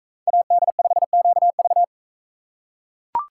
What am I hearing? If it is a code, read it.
Morse code, heard as a keyed tone, about 41 wpm, 700 Hz. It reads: AD5QV